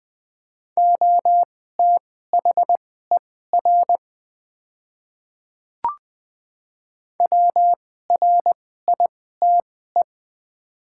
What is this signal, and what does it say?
Morse code, heard as a keyed tone, about 20 wpm, spaced as for 14 wpm, 700 Hz. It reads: OTHER WRITE
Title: 2x Character Spacing - Top 100 Words - 20wpm